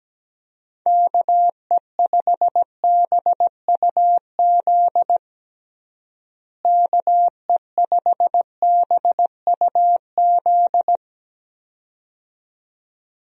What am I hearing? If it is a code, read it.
Morse code, heard as a keyed tone, about 17 wpm, 700 Hz. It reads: KE5BUZ KE5BUZ
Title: USA Call Signs - 6 Characters - Repeated 2x - 17wpm